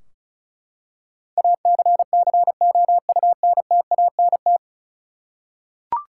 Morse code, heard as a keyed tone, about 36 wpm, 700 Hz. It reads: ACCOUNTANT
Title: Practice QSO - Exchange 3 - More Info - 35wpm